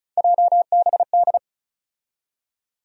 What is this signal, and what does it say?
Morse code, heard as a keyed tone, about 36 wpm, 700 Hz. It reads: JBD
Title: USA Call Sign Postfixes - 3 Letters 35wpm